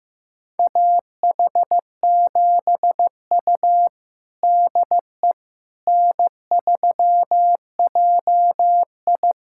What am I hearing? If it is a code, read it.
Morse code, heard as a keyed tone, about 16 wpm, 700 Hz. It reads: AH7U DE N3JI
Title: USA Callsign DE Callsign - 4 Characters - 15wpm